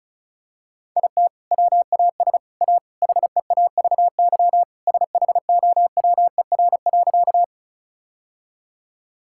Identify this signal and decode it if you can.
Morse code, heard as a keyed tone, about 36 wpm, 700 Hz. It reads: IT WAS A HEAVY SHOWER.
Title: Sentences from Top 1600 Words 35wpm